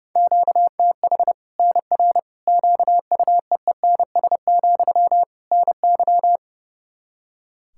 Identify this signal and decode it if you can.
Morse code, heard as a keyed tone, about 30 wpm, 700 Hz. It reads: QTH NR QUEENS, NY
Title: Practice QSO - Exchange 1 - Beginning - 30wpm